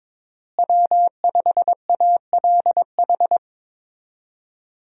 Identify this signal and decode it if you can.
Morse code, heard as a keyed tone, about 22 wpm, 700 Hz. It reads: W5ALH
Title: USA Call Signs - 5 Characters - Repeated 3x - 22wpm